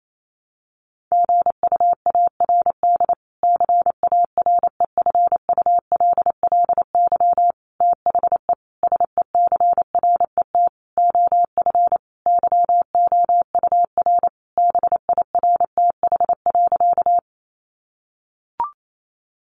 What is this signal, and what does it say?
Morse code, heard as a keyed tone, about 28 wpm, 700 Hz. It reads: GUARD CAREFULLY THE SECRET OF YOUR BIRTH.
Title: Sentences from Top 1300 Words 28wpm